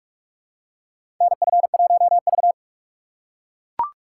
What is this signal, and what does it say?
Morse code, heard as a keyed tone, about 44 wpm, 700 Hz. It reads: NF1V
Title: USA Call Signs - 4 Characters - 2x Character Spacing - 45wpm